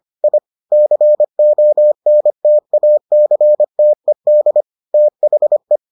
Morse code, heard as a keyed tone, about 25 wpm, 600 Hz. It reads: I CONTACTED THE